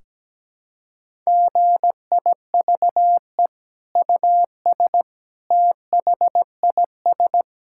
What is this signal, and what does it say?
Morse code, heard as a keyed tone, about 17 wpm, 700 Hz. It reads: GIVE US THIS
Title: Sentences from Top 1600 Words 17wpm